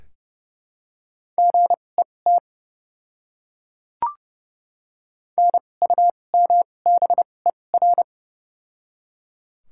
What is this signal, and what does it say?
Morse code, heard as a keyed tone, about 29 wpm, spaced as for 21 wpm, 700 Hz. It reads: GET NUMBER
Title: 2x Character Spacing - Top 300 Words - 30wpm